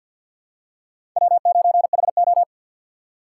Morse code, heard as a keyed tone, about 53 wpm, 700 Hz. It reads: W9HO
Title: USA Call Signs - 4 Characters - 3x Character Spacing - 50wpm